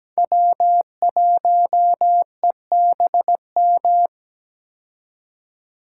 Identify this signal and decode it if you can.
Morse code, heard as a keyed tone, about 17 wpm, 700 Hz. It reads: W1EBM
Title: K1USN SST QSO Practice - 17wpm